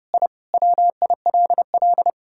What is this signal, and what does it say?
Morse code, heard as a keyed tone, about 30 wpm, 700 Hz. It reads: I WILL
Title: Sentences from Top 1000 Words - Review Part 5 - 30wpm